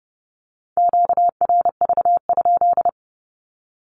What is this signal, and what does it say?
Morse code, heard as a keyed tone, about 30 wpm, 700 Hz. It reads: QRV?
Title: Q-Codes with Spoken Meaning - 30wpm